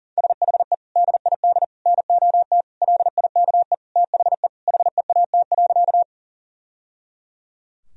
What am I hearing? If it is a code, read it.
Morse code, heard as a keyed tone, about 37 wpm, 700 Hz. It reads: SHE DID NOT LIKE THE HEAT.
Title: Sentences from Top 1000 Words - Review Part 3 - 40wpm